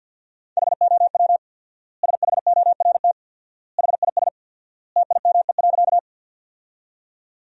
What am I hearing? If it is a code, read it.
Morse code, heard as a keyed tone, about 50 wpm, 700 Hz. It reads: HOW SHORT HIS TIME.
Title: 4x Word-Spacing - Sentences from Top 400 Words - 50wpm